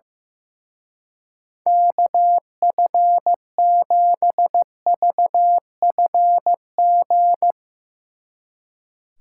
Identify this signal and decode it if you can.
Morse code, heard as a keyed tone, about 16 wpm, 700 Hz. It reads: KF7VFG
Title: Speed Racing - USA 2x3 Call Signs 15wpm